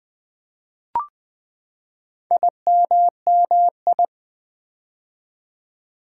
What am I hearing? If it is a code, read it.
Morse code, heard as a keyed tone, about 20 wpm, 700 Hz. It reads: IMMI
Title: Build A Word - 4 Letters At A Time - 8-Letter Words - 20wpm